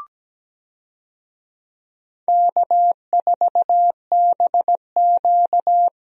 Morse code, heard as a keyed tone, about 17 wpm, 700 Hz. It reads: K4BQ
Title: USA Call Signs - 4 Characters - Repeated 2x - 17wpm